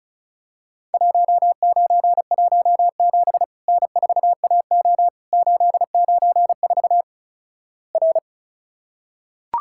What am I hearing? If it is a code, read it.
Morse code, heard as a keyed tone, about 36 wpm, 700 Hz. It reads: 1917 N4AO 894 R
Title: Worked All Europe DX Contest - QTC Exchange - 35wpm